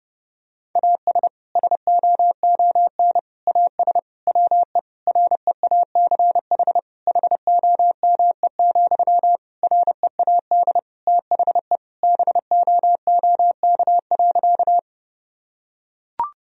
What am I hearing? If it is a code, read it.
Morse code, heard as a keyed tone, about 30 wpm, 700 Hz. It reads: AS SOON AS WE REACH HOME, READ THE BOOK.
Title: Sentences from Top 1000 Words - Review Part 2 - 30wpm